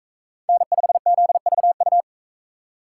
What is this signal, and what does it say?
Morse code, heard as a keyed tone, about 42 wpm, 700 Hz. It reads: N5ZVU